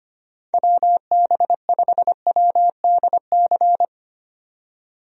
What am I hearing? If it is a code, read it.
Morse code, heard as a keyed tone, about 25 wpm, 700 Hz. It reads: WB5WDC
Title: USA Call Signs - 6 Characters - 25wpm